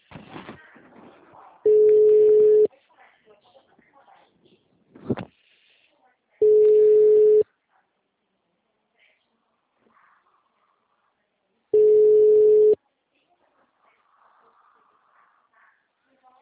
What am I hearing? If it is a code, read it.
Ringback tone (Europe, 425 Hz), with speech.